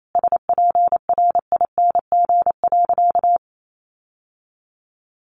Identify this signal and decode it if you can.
Morse code, heard as a keyed tone, about 28 wpm, 700 Hz. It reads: SPRING.
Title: Sentences from Top 1200 Words 28wpm